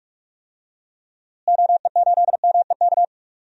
Morse code, heard as a keyed tone, about 46 wpm, 700 Hz. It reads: OE8MEK
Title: DXCC-100 Call Signs - 6 Characters - Repeated 2x - 45wpm